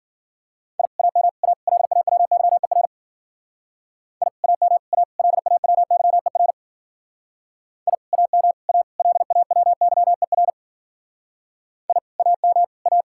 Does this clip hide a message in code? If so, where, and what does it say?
Yes: Morse code from 0.8 to 13.0 s, heard as a keyed tone, about 51 wpm, 700 Hz. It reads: I AM A LAWYER I AM A LAWYER I AM A LAWYER I AM A